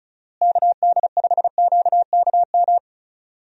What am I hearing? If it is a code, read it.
Morse code, heard as a keyed tone, about 34 wpm, 700 Hz. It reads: KD5QKM
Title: USA Call Signs - 6 Characters - Repeated 3x - 35wpm